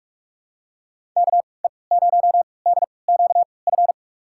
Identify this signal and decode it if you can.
Morse code, heard as a keyed tone, about 46 wpm, spaced as for 25 wpm, 700 Hz. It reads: KE0DQF
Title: USA Call Signs - 6 Characters - 3x Character Spacing - 45wpm